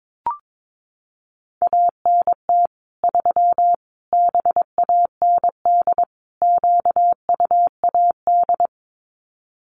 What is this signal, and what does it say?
Morse code, heard as a keyed tone, about 22 wpm, 700 Hz. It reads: ANT 3 BAND QUAD
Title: Common QSO Phrases 22wpm